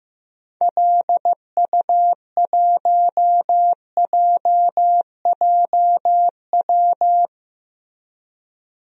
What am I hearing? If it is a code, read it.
Morse code, heard as a keyed tone, about 15 wpm, 700 Hz. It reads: LU1JJW
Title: DXCC-100 Call Signs - 6 Characters - Repeated 3x - 15wpm